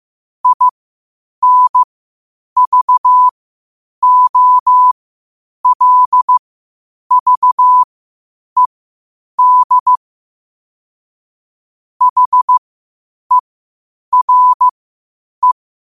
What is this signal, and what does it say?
Morse code, heard as a keyed tone, about 15 wpm, spaced as for 8 wpm, 1000 Hz. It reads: INVOLVED HERE
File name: qotdfiles / 8wpm